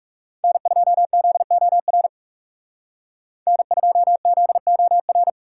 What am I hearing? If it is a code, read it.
Morse code, heard as a keyed tone, about 36 wpm, 700 Hz. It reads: N2ZOR N2ZOR
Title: Speed Racing - USA Call Signs 30wpm